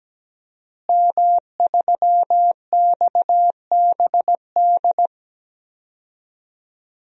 Morse code, heard as a keyed tone, about 17 wpm, 700 Hz. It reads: M3XBD
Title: DXCC-100 Call Signs - 5 Characters - Repeated 3x - 17wpm